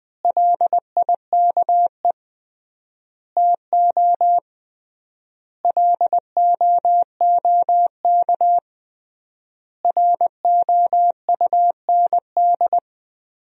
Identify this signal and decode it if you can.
Morse code, heard as a keyed tone, about 20 wpm, 700 Hz. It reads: LIKE TO LOOK ROUND
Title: 3x Word-Spacing - Sentences from Top 200 Words - 20wpm